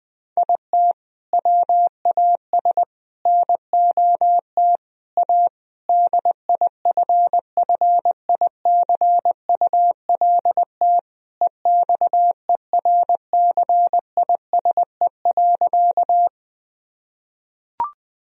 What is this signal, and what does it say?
Morse code, heard as a keyed tone, about 20 wpm, 700 Hz. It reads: IT WAS NOT A DIFFICULT EXERCISE.
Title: Sentences from Top 1000 Words - Review Part 4 - 20wpm